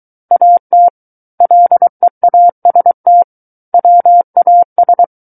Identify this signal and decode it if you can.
Morse code, heard as a keyed tone, about 23 wpm, 700 Hz. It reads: AT LEAST WAS
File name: Book One / 14-Chapter 14 - 23 WPM